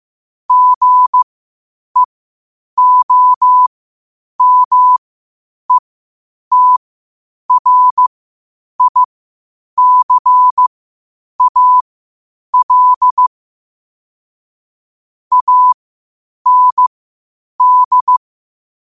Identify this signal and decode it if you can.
Morse code, heard as a keyed tone, about 15 wpm, spaced as for 8 wpm, 1000 Hz. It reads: GEOMETRICAL AND